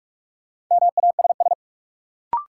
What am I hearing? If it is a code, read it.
Morse code, heard as a keyed tone, about 47 wpm, 700 Hz. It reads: MASS
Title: Sets of 5 Words Encore - New Word Drill - 45wpm